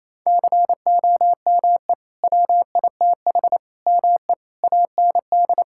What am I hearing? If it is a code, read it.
Morse code, heard as a keyed tone, about 28 wpm, 700 Hz. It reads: COME WITH ME AND
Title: Sentences from Top 1600 Words 28wpm